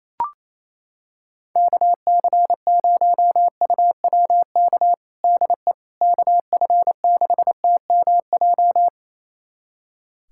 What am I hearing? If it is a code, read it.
Morse code, heard as a keyed tone, about 28 wpm, 700 Hz. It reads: KC0UWK DE KF6TMJ
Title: USA Callsign DE Callsign - 6 Characters - 28wpm